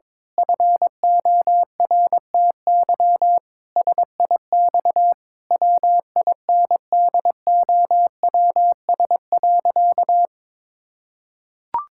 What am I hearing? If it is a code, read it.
Morse code, heard as a keyed tone, about 22 wpm, 700 Hz. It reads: FORTY SIX WINDOWS.
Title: Pangram Challenge v2 - 22wpm